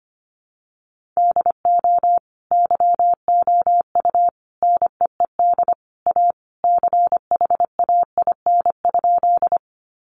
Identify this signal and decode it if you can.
Morse code, heard as a keyed tone, about 25 wpm, 700 Hz. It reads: DO YOU NEED A CHAIN?
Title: Sentences from Top 1500 Words 25wpm